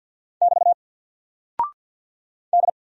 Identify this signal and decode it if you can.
Morse code, heard as a keyed tone, about 50 wpm, 700 Hz. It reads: <BT> D
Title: Single Character (Letter:Number:Punctuation:Prosign) 50wpm